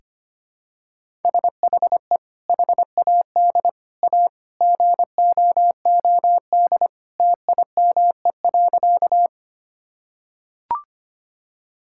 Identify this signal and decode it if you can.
Morse code, heard as a keyed tone, about 25 wpm, 700 Hz. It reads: SHE HAD A GOOD TIME.
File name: practice / 25wpm / Sentences with 2-Word Phrases - 25wpm